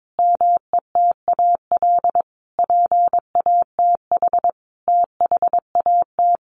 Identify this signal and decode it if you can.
Morse code, heard as a keyed tone, about 22 wpm, 700 Hz. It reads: METAL PATH THAT
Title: A Sound Of Thunder 22wpm